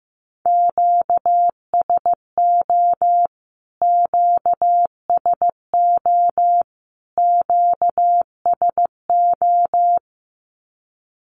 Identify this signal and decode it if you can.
Morse code, heard as a keyed tone, about 15 wpm, 700 Hz. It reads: QSO QSO QSO
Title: Q-Codes Repeated 3x 15wpm